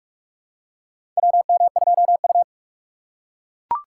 Morse code, heard as a keyed tone, about 44 wpm, 700 Hz. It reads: WM2U
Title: USA Call Signs - 4 Characters - 6x Character Spacing - 45wpm